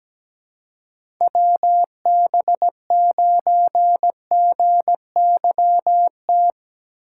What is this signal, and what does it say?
Morse code, heard as a keyed tone, about 17 wpm, 700 Hz. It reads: WB9GYT
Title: USA Call Signs - 6 Characters - 6x Character Spacing - 17wpm